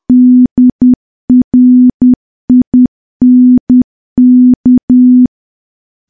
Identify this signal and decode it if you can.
Morse code, heard as a keyed tone, about 10 wpm, 250 Hz. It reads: DRINK
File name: morse / morse-training-10WPM